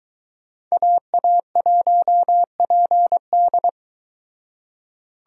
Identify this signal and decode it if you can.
Morse code, heard as a keyed tone, about 23 wpm, 700 Hz. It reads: AA1PD